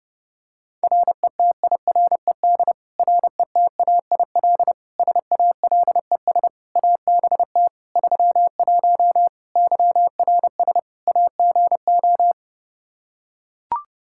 Morse code, heard as a keyed tone, about 30 wpm, 700 Hz. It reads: RETIRED RETAIL SALES ABT 31 YRS AGO
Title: Practice QSO - Exchange 3 - More Info - 30wpm